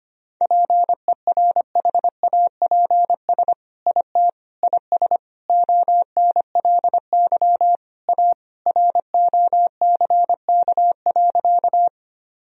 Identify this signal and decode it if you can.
Morse code, heard as a keyed tone, about 25 wpm, 700 Hz. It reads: PERHAPS IT IS ONLY A ROCK.
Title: Sentences from Top 1000 Words - Review Part 2 - 25wpm